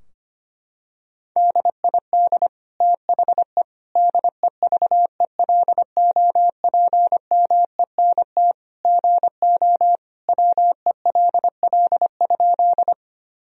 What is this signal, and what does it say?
Morse code, heard as a keyed tone, about 25 wpm, 700 Hz. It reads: DID THE DEVELOPMENT GO WELL?
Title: Sentences from Top 1000 Words - Review Part 4 - 25wpm